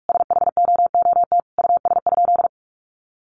Morse code, heard as a keyed tone, about 44 wpm, 700 Hz. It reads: SHOOT US?
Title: Sentences from Top 1200 Words 45wpm